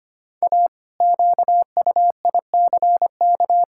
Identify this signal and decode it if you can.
Morse code, heard as a keyed tone, about 25 wpm, 700 Hz. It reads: A QUICK
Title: Pangram Challenge v2 - 25wpm